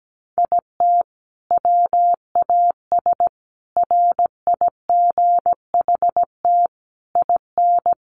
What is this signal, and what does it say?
Morse code, heard as a keyed tone, about 17 wpm, 700 Hz. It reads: IT WAS RIGHT IN
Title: Sentences from Top 500 Words 17wpm